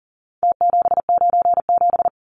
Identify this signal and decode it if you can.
Morse code, heard as a keyed tone, about 40 wpm, 700 Hz. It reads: T797
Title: Numbers - Contest Style - 4-Digit 40wpm